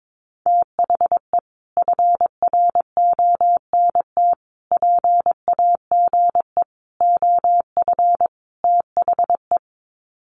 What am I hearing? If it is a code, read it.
Morse code, heard as a keyed tone, about 22 wpm, 700 Hz. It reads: THE FRONT PAGE OF THE
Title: Sets of 5 Words Encore 22wpm